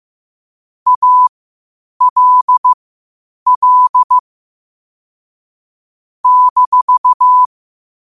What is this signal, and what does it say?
Morse code, heard as a keyed tone, about 15 wpm, spaced as for 8 wpm, 1000 Hz. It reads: ALL -
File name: qotdfiles / 8wpm